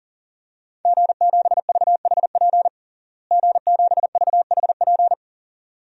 Morse code, heard as a keyed tone, about 40 wpm, 700 Hz. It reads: G7VHP G7VHP
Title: DXCC-100 Call Signs - 5 Characters - Repeated 2x - 40wpm